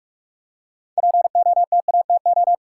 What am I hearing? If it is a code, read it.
Morse code, heard as a keyed tone, about 42 wpm, 700 Hz. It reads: POTATO
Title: Six Letter Words - 45wpm